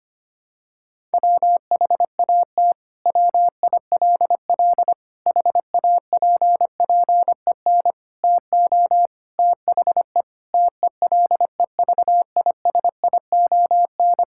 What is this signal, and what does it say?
Morse code, heard as a keyed tone, about 25 wpm, 700 Hz. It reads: WHAT WILL HAPPEN TO THE TELEVISION
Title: Sentences from Top 1000 Words - Review Part 5 - 25wpm